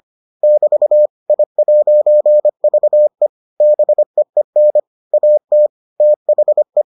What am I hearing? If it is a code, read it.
Morse code, heard as a keyed tone, about 25 wpm, 600 Hz. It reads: <BT> I'VE BEEN AT THE